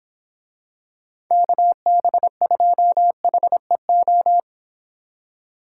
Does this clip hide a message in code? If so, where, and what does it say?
Yes: Morse code from 1.3 to 4.4 s, heard as a keyed tone, about 25 wpm, 700 Hz. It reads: KB2HEO